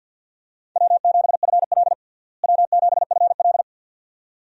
Morse code, heard as a keyed tone, about 52 wpm, 700 Hz. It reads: W7FL W7FL
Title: USA Call Signs - 4 Characters - Repeated 2x - 50wpm